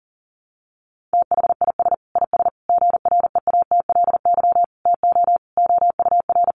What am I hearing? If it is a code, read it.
Morse code, heard as a keyed tone, about 40 wpm, 700 Hz. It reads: THIS IS GREATLY TO OUR
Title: Sentences from Top 1400 Words 40wpm